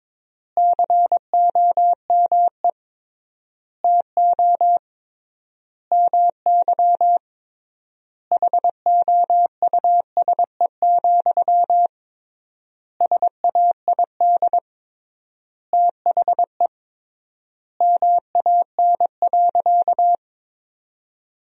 Morse code, heard as a keyed tone, about 22 wpm, 700 Hz. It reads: COME TO MY HOUSE, SAID THE MAN.
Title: 3x Word-Spacing - Sentences from Top 400 Words - 22wpm